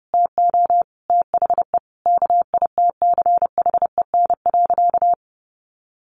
Morse code, heard as a keyed tone, about 30 wpm, 700 Hz. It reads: TO THE KITCHEN.
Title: Sentences from Top 1400 Words 30wpm